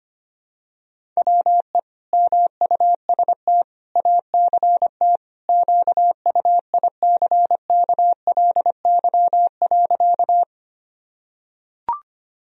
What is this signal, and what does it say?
Morse code, heard as a keyed tone, about 25 wpm, 700 Hz. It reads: WE MUST ACT QUICKLY.
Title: Sentences from Top 1000 Words - Review Part 1 - 25wpm